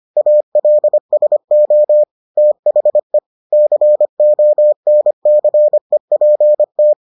Morse code, heard as a keyed tone, about 25 wpm, 600 Hz. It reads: ALSO THE CONCEPT